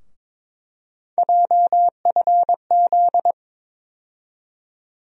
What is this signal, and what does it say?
Morse code, heard as a keyed tone, about 22 wpm, 700 Hz. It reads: JFZ
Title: USA Call Sign Postfixes - 3 Letters 22wpm